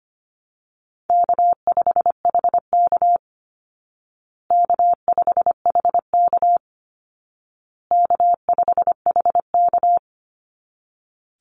Morse code, heard as a keyed tone, about 25 wpm, 700 Hz. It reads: K5HK K5HK K5HK
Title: Call Signs Repeated 2x - 25wpm